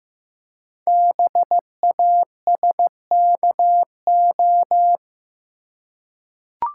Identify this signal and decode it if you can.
Morse code, heard as a keyed tone, about 15 wpm, 700 Hz. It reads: BASKO
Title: Code Groups - 5 Letters - 15wpm